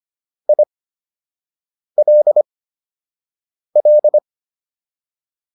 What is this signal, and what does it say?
Morse code, heard as a keyed tone, about 25 wpm, spaced as for 6 wpm, 600 Hz. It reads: ILL